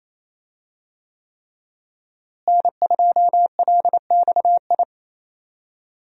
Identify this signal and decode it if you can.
Morse code, heard as a keyed tone, about 28 wpm, 700 Hz. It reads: N2LXI
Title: USA Call Signs - 5 Characters - Repeated 3x - 28wpm